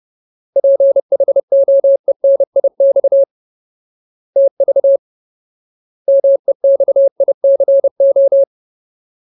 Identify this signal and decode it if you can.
Morse code, heard as a keyed tone, about 30 wpm, 550 Hz. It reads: PHOENIX TV MEXICO
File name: Ultimate Word List / 30WPM Ultimate